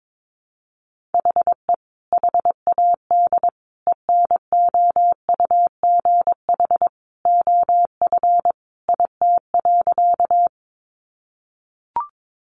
Morse code, heard as a keyed tone, about 22 wpm, 700 Hz. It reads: HE HAD ENOUGH OF IT.
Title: Sentences from Top 400 Words 22wpm